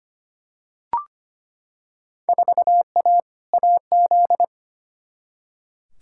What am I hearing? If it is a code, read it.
Morse code, heard as a keyed tone, about 25 wpm, 700 Hz. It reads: 4A AZ
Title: Field Day Exchanges - 25wpm